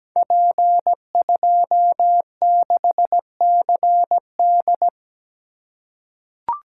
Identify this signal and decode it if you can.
Morse code, heard as a keyed tone, about 17 wpm, 700 Hz. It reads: P26CD
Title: Code Groups - 5 Letter:Numbers - 17wpm